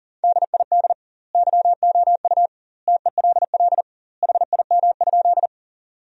Morse code, heard as a keyed tone, about 40 wpm, 700 Hz. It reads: DID YOU TELL HIM?
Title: 2x Word-Spacing - Sentences from Top 200 Words - 40wpm